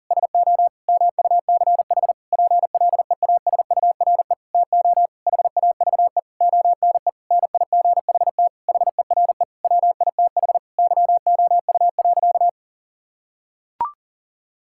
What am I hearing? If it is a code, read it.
Morse code, heard as a keyed tone, about 40 wpm, 700 Hz. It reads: SO MUCH PLEASURE TO HAVE ONE NIGHT HERE WITH YOU.
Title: Sentences from Top 1600 Words 40wpm